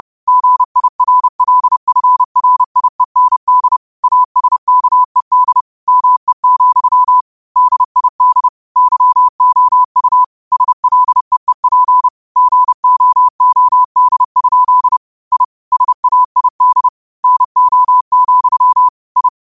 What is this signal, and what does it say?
Morse code, heard as a keyed tone, about 30 wpm, 1000 Hz. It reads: GIRLFRIEND ASKED ME, DID YOU SLEEP GOOD? I SAID NO, I
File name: qotdfiles / 30wpm